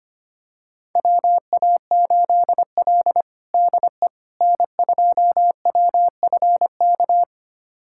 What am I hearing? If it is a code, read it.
Morse code, heard as a keyed tone, about 25 wpm, 700 Hz. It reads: WA8L DE N2WFK
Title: USA Callsign DE Callsign 25wpm